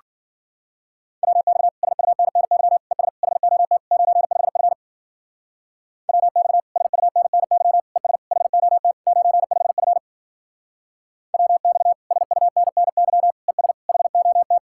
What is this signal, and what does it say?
Morse code, heard as a keyed tone, about 50 wpm, 700 Hz. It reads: WX SUNNY ES HOT 95F WX SUNNY ES HOT 95F WX SUNNY ES HOT